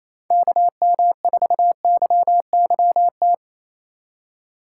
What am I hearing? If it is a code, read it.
Morse code, heard as a keyed tone, about 34 wpm, 700 Hz. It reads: KM4YYT
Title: Speed Racing - USA 2x3 Call Signs 28wpm